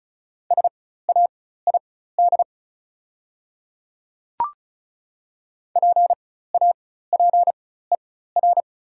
Morse code, heard as a keyed tone, about 36 wpm, spaced as for 16 wpm, 690 Hz. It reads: SAID PAPER
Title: 4x Character Spacing - Top 300 Words - 35wpm